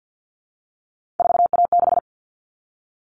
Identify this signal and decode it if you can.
Morse code, heard as a keyed tone, about 50 wpm, 700 Hz. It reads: 4A6
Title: All Cut Numbers - 3-Digit 50wpm